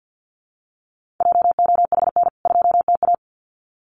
Morse code, heard as a keyed tone, about 50 wpm, 700 Hz. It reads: JOHN 2TA
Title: CWT Exchange 50wpm